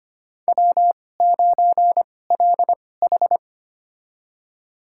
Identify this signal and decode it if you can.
Morse code, heard as a keyed tone, about 25 wpm, spaced as for 18 wpm, 700 Hz. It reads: W9LH
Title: USA Call Signs - 4 Characters - 2x Character Spacing - 25wpm